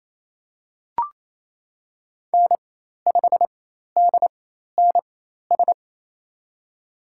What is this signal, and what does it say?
Morse code, heard as a keyed tone, about 28 wpm, spaced as for 13 wpm, 700 Hz. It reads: N5DNS
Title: USA Call Signs - 5 Characters - 4x Character Spacing - 28wpm